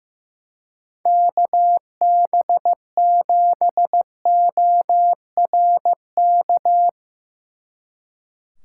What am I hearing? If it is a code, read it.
Morse code, heard as a keyed tone, about 15 wpm, 700 Hz. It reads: KB7ORK
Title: Speed Racing - USA 2x3 Call Signs 15wpm